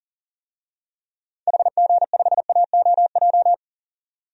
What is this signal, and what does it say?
Morse code, heard as a keyed tone, about 40 wpm, 700 Hz. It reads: HG5AOJ
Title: DXCC-100 Call Signs - 6 Characters - 40wpm